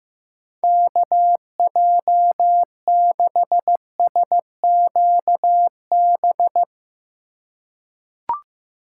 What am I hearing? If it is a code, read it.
Morse code, heard as a keyed tone, about 15 wpm, 700 Hz. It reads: KJ6SQB
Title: USA Call Signs - 6 Characters - 2x Character Spacing - 15wpm